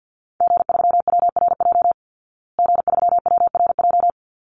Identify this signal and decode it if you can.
Morse code, heard as a keyed tone, about 50 wpm, 700 Hz. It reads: G3WRJ G3WRJ
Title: Call Signs Repeated 3x - 50wpm